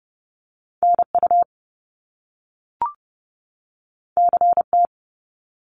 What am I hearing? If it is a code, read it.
Morse code, heard as a keyed tone, about 30 wpm, 700 Hz. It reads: NU CT
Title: Top Two Letter Combinations in Words 30wpm